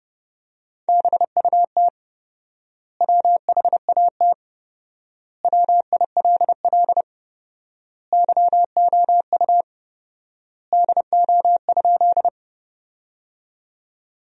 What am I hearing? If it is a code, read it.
Morse code, heard as a keyed tone, about 30 wpm, 700 Hz. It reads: BUT WHAT WILL YOU DO?
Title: 4x Word-Spacing - Sentences from Top 100 Words - 30wpm